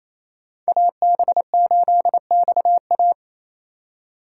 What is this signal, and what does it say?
Morse code, heard as a keyed tone, about 28 wpm, 700 Hz. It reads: AB8XA
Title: USA Call Signs - 5 Characters - 4x Character Spacing - 28wpm